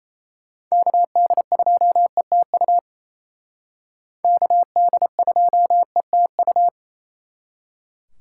Morse code, heard as a keyed tone, about 36 wpm, 700 Hz. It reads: KD2ETU KD2ETU